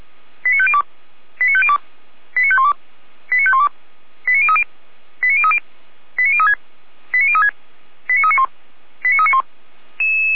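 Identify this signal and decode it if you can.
Five-tone selective call (ZVEI), heard as quick runs of five tones, tones of about 70 ms. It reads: 89502 89502 89512 89512 89030 89030 89036 89036 89391 89391
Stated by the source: ZVEI five-tone selective call received on an SDR.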